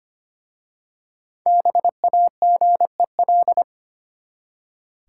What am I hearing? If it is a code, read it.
Morse code, heard as a keyed tone, about 25 wpm, 700 Hz. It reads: BAGEL
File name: practice / 25wpm / Easy Words with Hints - 25wpm